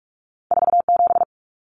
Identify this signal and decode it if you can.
Morse code, heard as a keyed tone, about 46 wpm, 700 Hz. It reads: 47
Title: Numbers - 2-Digit 45wpm